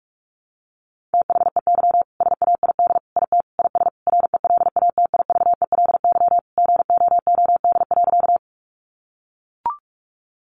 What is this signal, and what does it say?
Morse code, heard as a keyed tone, about 44 wpm, 700 Hz. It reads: THEY SAID IT IS RELATIVELY GOOD.
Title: Sentences from Top 1100 Words 45wpm